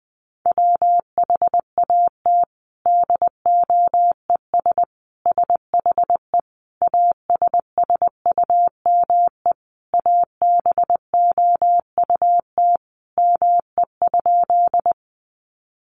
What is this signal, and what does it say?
Morse code, heard as a keyed tone, about 20 wpm, 700 Hz. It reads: WHAT DOES SHE ASSUME ABOUT ME?
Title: Sentences from Top 1500 Words 20wpm